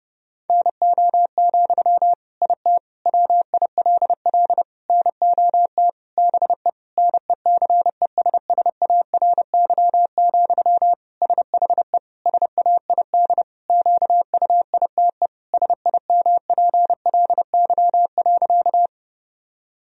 Morse code, heard as a keyed tone, about 30 wpm, 700 Hz. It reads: NO, IT WILL NOT BE NECESSARY, SHE SAID QUITE SIMPLY.
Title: Sentences from Top 1000 Words - Review Part 4 - 30wpm